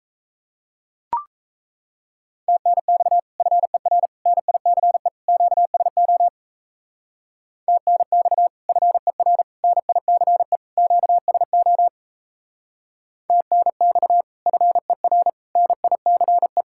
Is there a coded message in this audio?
Yes: Morse code from 2.5 to 16.6 s, heard as a keyed tone, about 36 wpm, 700 Hz. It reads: TNX FER NICE QSO TNX FER NICE QSO TNX FER NICE